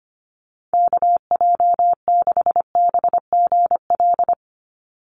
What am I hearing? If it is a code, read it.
Morse code, heard as a keyed tone, about 25 wpm, 700 Hz. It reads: KJ6BGL
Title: USA Call Signs 25wpm